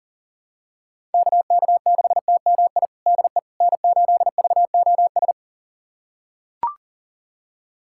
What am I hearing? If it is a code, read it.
Morse code, heard as a keyed tone, about 40 wpm, 700 Hz. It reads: KK6TMI DE N8VOS
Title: USA Callsign DE Callsign 40wpm